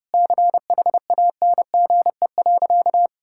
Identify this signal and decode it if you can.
Morse code, heard as a keyed tone, about 32 wpm, 700 Hz. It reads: CHANGE.
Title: Sentences from Top 1000 Words - Review Part 5 - 30wpm